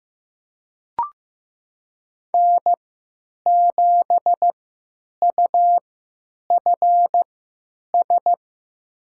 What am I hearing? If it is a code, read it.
Morse code, heard as a keyed tone, about 15 wpm, spaced as for 9 wpm, 700 Hz. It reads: N7UFS
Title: USA Call Signs - 5 Characters - 3x Character Spacing - 15wpm